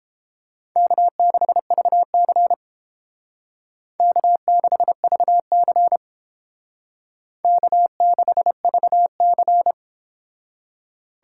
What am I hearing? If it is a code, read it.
Morse code, heard as a keyed tone, about 29 wpm, 700 Hz. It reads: K6VC K6VC K6VC